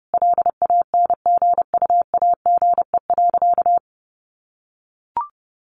Morse code, heard as a keyed tone, about 30 wpm, 700 Hz. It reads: LANGUAGE.
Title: Sentences from Top 1300 Words 30wpm